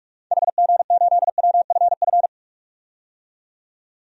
Morse code, heard as a keyed tone, about 47 wpm, 700 Hz. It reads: HG8WFF
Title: DXCC-100 Call Signs - 6 Characters - 45wpm